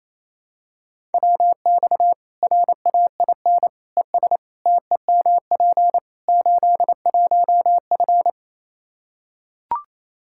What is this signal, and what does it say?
Morse code, heard as a keyed tone, about 27 wpm, 700 Hz. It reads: WX RAIN ES TEMP 81F
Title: Practice QSO - Exchange 2 - Info - 28wpm